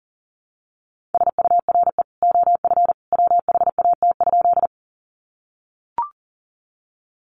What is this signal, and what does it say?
Morse code, heard as a keyed tone, about 40 wpm, 700 Hz. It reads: SURE OF WHAT?
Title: Sentences from Top 300 Words 40wpm